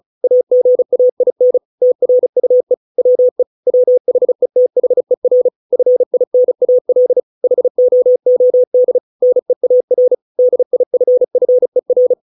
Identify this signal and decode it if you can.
Morse code, heard as a keyed tone, about 34 wpm, 500 Hz. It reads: AGAIN TRUE WE WHETHER FINAL HOOD NEAR DIFFER
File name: 632 words / 632 Words 35WPM